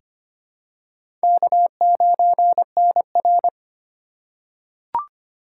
Morse code, heard as a keyed tone, about 25 wpm, 700 Hz. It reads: K9NR
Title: K1USN SST QSO Practice - 25wpm